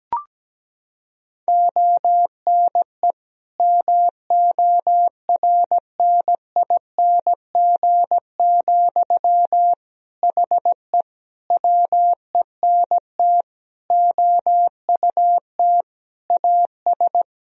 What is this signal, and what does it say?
Morse code, heard as a keyed tone, about 17 wpm, 700 Hz. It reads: ONE MORNING, HE WENT OUT AS
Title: Sentences from Top 1000 Words - Review Part 2 - 17wpm